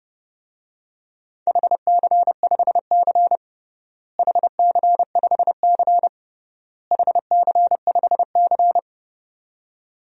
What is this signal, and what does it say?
Morse code, heard as a keyed tone, about 32 wpm, 700 Hz. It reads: HC5C HC5C HC5C
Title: DXCC-100 Call Signs - 4 Characters - Repeated 3x - 30wpm